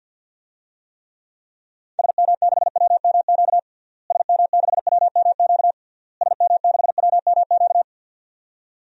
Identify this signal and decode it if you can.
Morse code, heard as a keyed tone, about 50 wpm, 700 Hz. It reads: SM6WMQ SM6WMQ SM6WMQ
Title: DXCC-100 Call Signs - 6 Characters - Repeated 3x - 50wpm